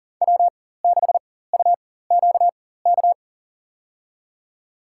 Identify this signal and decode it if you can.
Morse code, heard as a keyed tone, about 40 wpm, spaced as for 19 wpm, 700 Hz. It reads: W6UQK
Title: USA Call Signs - 5 Characters - 4x Character Spacing - 40wpm